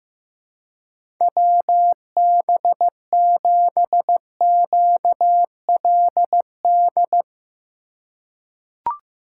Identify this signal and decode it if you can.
Morse code, heard as a keyed tone, about 15 wpm, 700 Hz. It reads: WB7QLD
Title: USA Call Signs - 6 Characters - 3x Character Spacing - 15wpm